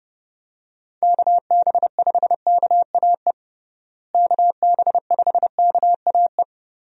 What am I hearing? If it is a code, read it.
Morse code, heard as a keyed tone, about 32 wpm, 700 Hz. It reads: KB5KAE KB5KAE
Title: USA Call Signs - 6 Characters - Repeated 2x - 30wpm